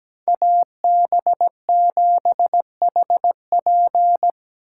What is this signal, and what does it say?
Morse code, heard as a keyed tone, about 20 wpm, 700 Hz. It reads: AB7HP
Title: Speed Racing - USA 2x2 Call Signs 17wpm